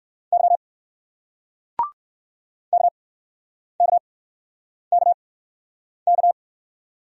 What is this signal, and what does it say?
Morse code, heard as a keyed tone, about 57 wpm, spaced as for 9 wpm, 700 Hz. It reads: Y KKKK